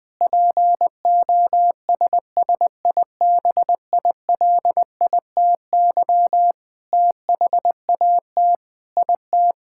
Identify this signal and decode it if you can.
Morse code, heard as a keyed tone, about 20 wpm, 700 Hz. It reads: POSSIBILITY THAT IT
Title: Sentences with 8-Word Phrases - 20wpm